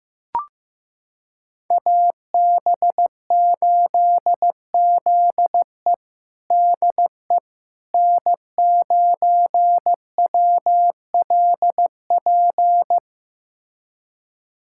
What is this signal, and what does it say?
Morse code, heard as a keyed tone, about 16 wpm, 700 Hz. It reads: AB8ZE DE N9WLP
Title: USA Callsign DE Callsign - 5 Characters - 15wpm